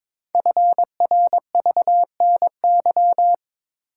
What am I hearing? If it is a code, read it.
Morse code, heard as a keyed tone, about 23 wpm, 700 Hz. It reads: FRVNY
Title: Code Groups - 5 Letters - 22wpm